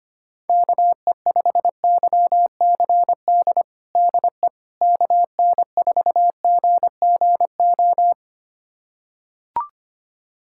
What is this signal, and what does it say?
Morse code, heard as a keyed tone, about 25 wpm, 700 Hz. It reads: KE5YCD DE KN4GGO
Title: USA Callsign DE Callsign - 6 Characters - 25wpm